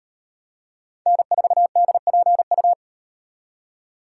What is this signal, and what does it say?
Morse code, heard as a keyed tone, about 38 wpm, 700 Hz. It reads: N4DPU